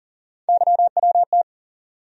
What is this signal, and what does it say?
Morse code, heard as a keyed tone, about 40 wpm, 700 Hz. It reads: YWT
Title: USA Call Sign Postfixes - 3 Letters 40wpm